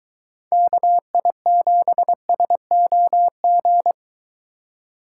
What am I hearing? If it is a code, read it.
Morse code, heard as a keyed tone, about 24 wpm, 700 Hz. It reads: KI7SOG